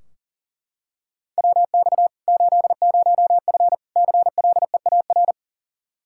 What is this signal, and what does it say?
Morse code, heard as a keyed tone, about 40 wpm, 700 Hz. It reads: WX 80F CLEAR
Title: Practice QSO - Exchange 2 - Info - 40wpm